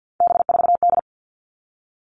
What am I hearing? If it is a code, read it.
Morse code, heard as a keyed tone, about 50 wpm, 700 Hz. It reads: B4D
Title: All Cut Numbers - 3-Digit 50wpm